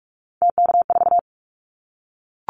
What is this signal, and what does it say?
Morse code, heard as a keyed tone, about 44 wpm, 700 Hz. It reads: TK4
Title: Callsign Prefixes for Top 100 DXCC - 45wpm